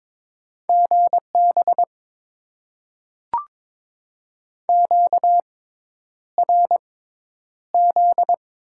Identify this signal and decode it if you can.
Morse code, heard as a keyed tone, about 22 wpm, spaced as for 10 wpm, 700 Hz. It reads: GB QRZ
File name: Top 90 QSO Elements - 6x Character Spacing - 22wpm